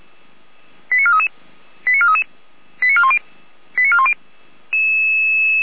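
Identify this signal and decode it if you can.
Five-tone selective call (ZVEI), heard as quick runs of five tones, tones of about 70 ms. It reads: 89433 89433 89410 89410
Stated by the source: ZVEI five-tone selective call received with an SDR.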